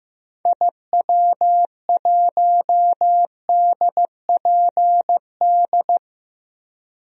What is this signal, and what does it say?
Morse code, heard as a keyed tone, about 15 wpm, 700 Hz. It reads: IW1DPD
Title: DXCC-100 Call Signs - 6 Characters - Repeated 2x - 15wpm